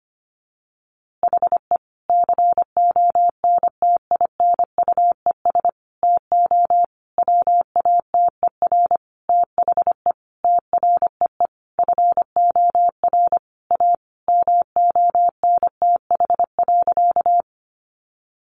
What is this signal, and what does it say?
Morse code, heard as a keyed tone, about 25 wpm, 700 Hz. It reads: HE CONTINUES TO WATER THE TREE FOR A MONTH.
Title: Easy Phrases 25wpm